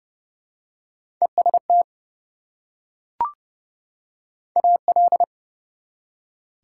Morse code, heard as a keyed tone, about 29 wpm, 700 Hz. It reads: EST AL
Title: Prefixes and Suffixes 30wpm